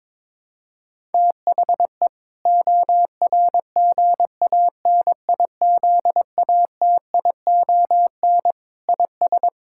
Morse code, heard as a keyed tone, about 22 wpm, 700 Hz. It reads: THE ORGANIZATION IS
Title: Sentences from Top 1000 Words - Review Part 5 - 22wpm